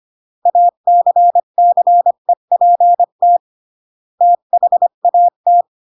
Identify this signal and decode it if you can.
Morse code, heard as a keyed tone, about 25 wpm, 700 Hz. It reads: ACCEPT THAT